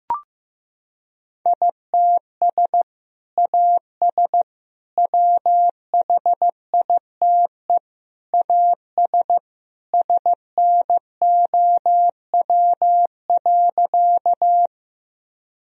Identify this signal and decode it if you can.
Morse code, heard as a keyed tone, about 15 wpm, 700 Hz. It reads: ITS AS WHITE AS SNOW.
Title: Sentences from Top 1000 Words - Review Part 3 - 15wpm